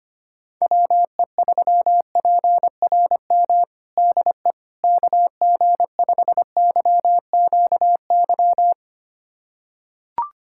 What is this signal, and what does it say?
Morse code, heard as a keyed tone, about 25 wpm, 700 Hz. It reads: WE3PRM DE KG5YQY
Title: USA Callsign DE Callsign - 6 Characters - 25wpm